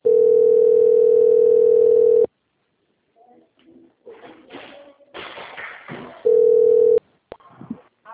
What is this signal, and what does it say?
Ringback tone (North America), 440 + 480 Hz, with speech.
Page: Araraquara - SP